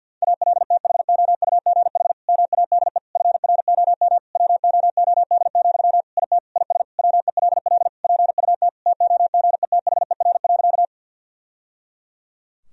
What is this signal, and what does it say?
Morse code, heard as a keyed tone, about 52 wpm, 700 Hz. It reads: ALTHOUGH MADE FROM WOOD, IT IS WELL PUT TOGETHER.
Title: Sentences from Top 1000 Words - Review Part 3 - 50wpm